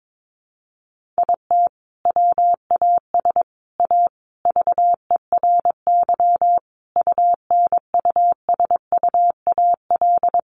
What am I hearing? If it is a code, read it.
Morse code, heard as a keyed tone, about 22 wpm, 700 Hz. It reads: IT WAS A VERY UNUSUAL
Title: Sentences from Top 1400 Words 22wpm